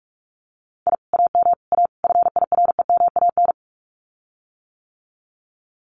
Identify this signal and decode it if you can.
Morse code, heard as a keyed tone, about 46 wpm, 700 Hz. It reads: I AM A FIREMAN
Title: Common QSO Phrases Encore 45wpm